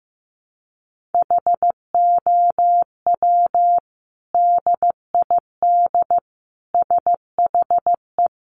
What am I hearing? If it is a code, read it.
Morse code, heard as a keyed tone, about 15 wpm, 700 Hz. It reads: HOW DID SHE
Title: Sentences from Top 1300 Words 15wpm